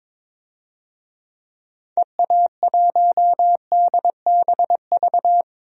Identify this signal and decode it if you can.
Morse code, heard as a keyed tone, about 22 wpm, 700 Hz. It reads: EA1DBV
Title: DXCC-100 Call Signs - 6 Characters - Repeated 3x - 22wpm